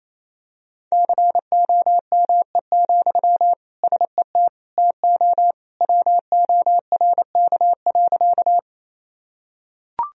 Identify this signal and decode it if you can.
Morse code, heard as a keyed tone, about 27 wpm, 700 Hz. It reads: COME, SET TO WORK.
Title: Sentences from Top 1000 Words - Review Part 1 - 28wpm